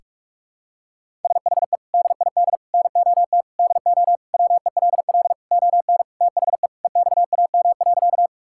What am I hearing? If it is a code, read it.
Morse code, heard as a keyed tone, about 41 wpm, 700 Hz. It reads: SHE DID NOT DO WELL ON THE EXAM.
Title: Sentences from Top 1000 Words - Review Part 5 - 45wpm